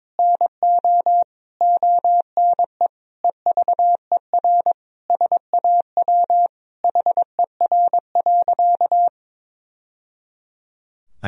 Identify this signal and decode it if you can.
Morse code, heard as a keyed tone, about 22 wpm, 700 Hz. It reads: NO ONE EVER SAW HER.
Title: Sentences from Top 1000 Words - Review Part 2 - 22wpm